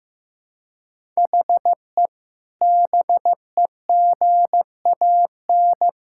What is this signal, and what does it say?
Morse code, heard as a keyed tone, about 15 wpm, 700 Hz. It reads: HE BEGAN
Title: Sentences from Top 1600 Words 15wpm